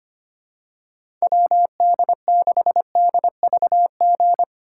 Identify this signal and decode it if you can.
Morse code, heard as a keyed tone, about 25 wpm, 700 Hz. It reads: WD6DVG
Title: USA Call Signs - 6 Characters - 4x Character Spacing - 25wpm